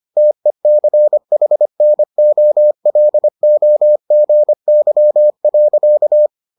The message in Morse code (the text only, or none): TECHNOLOGY.